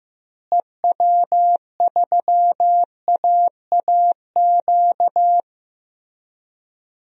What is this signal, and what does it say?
Morse code, heard as a keyed tone, about 16 wpm, 700 Hz. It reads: EW3AAQ
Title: DXCC-100 Call Signs - 6 Characters - Repeated 2x - 15wpm